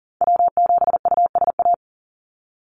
Morse code, heard as a keyed tone, about 40 wpm, 700 Hz. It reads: W7USA
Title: NA SOTA Chaser Callsigns 40wpm